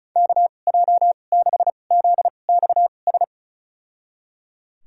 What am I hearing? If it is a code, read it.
Morse code, heard as a keyed tone, about 36 wpm, spaced as for 25 wpm, 700 Hz. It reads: KJ6ZXS
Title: USA Call Signs - 2x Character Spacing - 35wpm